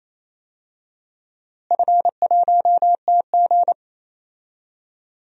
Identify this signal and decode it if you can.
Morse code, heard as a keyed tone, about 28 wpm, 700 Hz. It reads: F1TG
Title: DXCC-100 Call Signs - 4 Characters - Repeated 2x - 28wpm